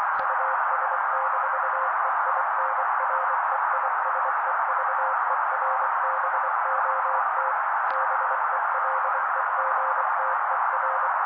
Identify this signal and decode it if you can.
Morse code, heard as a keyed tone, about 24 wpm, 530 Hz. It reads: US-EINREISEVERBOT BELEGTER
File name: daily / 23wpm noise